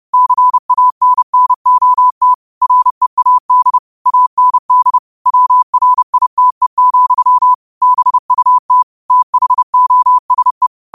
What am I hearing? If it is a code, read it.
Morse code, heard as a keyed tone, about 30 wpm, 1000 Hz. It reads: CANNOT READ AND WRITE, BUT THOSE
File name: qotdfiles / 30wpm